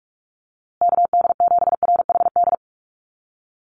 Morse code, heard as a keyed tone, about 44 wpm, 700 Hz. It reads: KD7RHD
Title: USA Call Signs Repeated 3x 45wpm